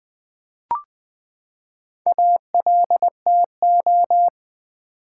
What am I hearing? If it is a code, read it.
Morse code, heard as a keyed tone, about 20 wpm, 700 Hz. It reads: ALTO
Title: Build A Word - 4 Letters At A Time - 10-Letter Words - 20wpm